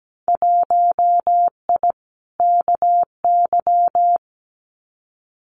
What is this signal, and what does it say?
Morse code, heard as a keyed tone, about 17 wpm, 700 Hz. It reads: 1I KY
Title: Winter Field Day Exchanges 17wpm